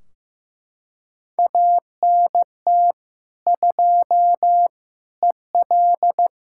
Morse code, heard as a keyed tone, about 15 wpm, 700 Hz. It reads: ANT 2 EL
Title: Practice QSO - Exchange 2 - Info - 15wpm